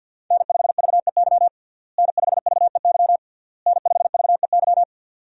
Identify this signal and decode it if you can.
Morse code, heard as a keyed tone, about 49 wpm, 700 Hz. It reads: N5VEY N5VEY N5VEY
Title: USA Call Signs - 5 Characters - Repeated 3x - 50wpm